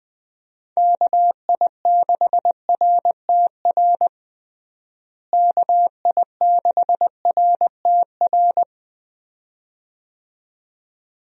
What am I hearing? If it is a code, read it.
Morse code, heard as a keyed tone, about 21 wpm, 700 Hz. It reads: KI6RTR KI6RTR
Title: USA Call Signs - 6 Characters - Repeated 2x - 20wpm